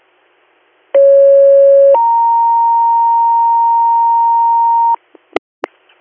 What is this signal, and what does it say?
Two-tone page, heard as a one-second tone, then a three-second tone, about 550 then 930 Hz, with noise. It is heard from 0.9 to 5.0 s.